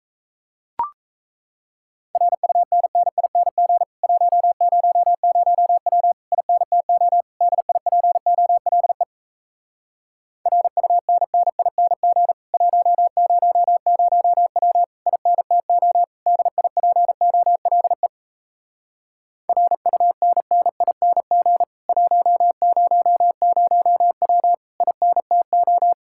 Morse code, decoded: RUNNING 100W INTO DIPOLE RUNNING 100W INTO DIPOLE RUNNING 100W INTO

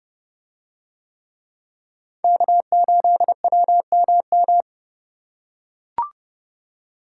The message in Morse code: K8WMM

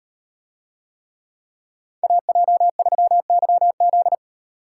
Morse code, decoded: AJ3YZ

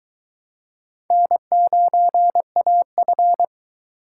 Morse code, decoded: N9AF

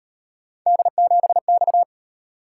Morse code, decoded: D7X

